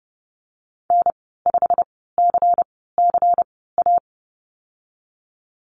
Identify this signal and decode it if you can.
Morse code, heard as a keyed tone, about 30 wpm, spaced as for 17 wpm, 700 Hz. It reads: N5CCA
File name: Lesson 37g - Letter x - Call Signs 30wpm